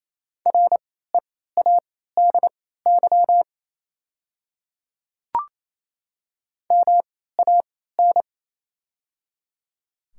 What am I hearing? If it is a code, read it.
Morse code, heard as a keyed tone, about 28 wpm, spaced as for 16 wpm, 700 Hz. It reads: READY MAN
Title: 3x Character Spacing - Top 500 Words - 28wpm